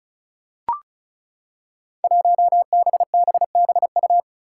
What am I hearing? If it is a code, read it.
Morse code, heard as a keyed tone, about 34 wpm, 700 Hz. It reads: 1BBBU
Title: Five Challenging Letter-Number Combinations 35wpm